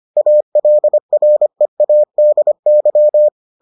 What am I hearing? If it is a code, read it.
Morse code, heard as a keyed tone, about 25 wpm, 600 Hz. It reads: ALREADY